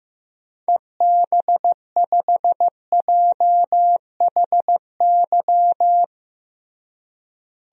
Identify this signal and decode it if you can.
Morse code, heard as a keyed tone, about 15 wpm, 700 Hz. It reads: EB5JHY
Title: DXCC-100 Call Signs - 6 Characters - Repeated 3x - 15wpm